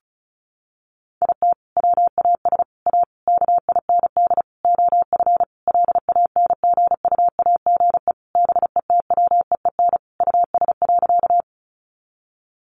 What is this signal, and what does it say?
Morse code, heard as a keyed tone, about 34 wpm, 700 Hz. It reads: IT WAS A KIND OF LANGUAGE BETWEEN US.
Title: Sentences from Top 500 Words 35wpm